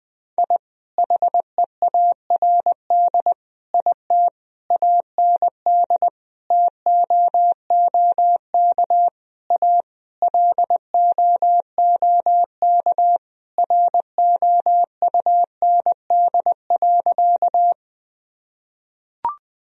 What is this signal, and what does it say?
Morse code, heard as a keyed tone, about 21 wpm, 700 Hz. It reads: I HEARD IT AND TOOK A LOOK ROUND.
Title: Sentences from Top 1000 Words - Review Part 1 - 20wpm